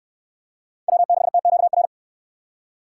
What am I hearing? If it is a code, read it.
Morse code, heard as a keyed tone, about 69 wpm, 700 Hz. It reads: W6TQR